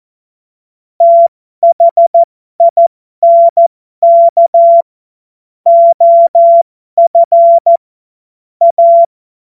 Morse code, heard as a keyed tone, about 14 wpm, spaced as for 12 wpm, 680 Hz. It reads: THINK OF A
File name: Morsekurs / Witze